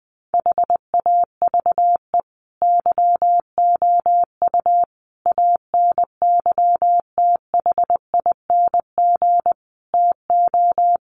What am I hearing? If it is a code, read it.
Morse code, heard as a keyed tone, about 20 wpm, 700 Hz. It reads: HAVE YOU ANYTHING TO
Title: Sentences from Top 1100 Words 20wpm